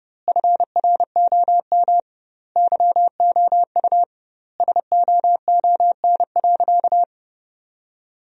Morse code, heard as a keyed tone, about 32 wpm, 700 Hz. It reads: FROM YOU SOON.
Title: 2x Word-Spacing - Sentences from Top 400 Words - 30wpm